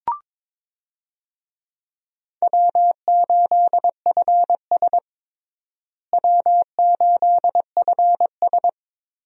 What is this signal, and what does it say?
Morse code, heard as a keyed tone, about 22 wpm, 700 Hz. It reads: W8FS W8FS
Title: USA Call Signs - 4 Characters - Repeated 3x - 22wpm